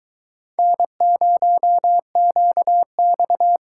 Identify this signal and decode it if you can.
Morse code, heard as a keyed tone, about 24 wpm, 700 Hz. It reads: N0QX